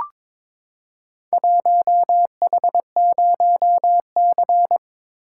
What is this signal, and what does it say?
Morse code, heard as a keyed tone, about 22 wpm, 700 Hz. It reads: 1H0C
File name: practice / 22wpm / Four Letter-Number Combinations - 22wpm